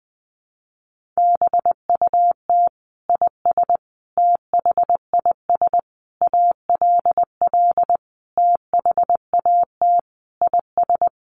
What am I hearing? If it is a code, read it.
Morse code, heard as a keyed tone, about 20 wpm, 700 Hz. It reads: BUT IS THIS ALL THAT IS
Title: Sentences from Top 600 Words 20wpm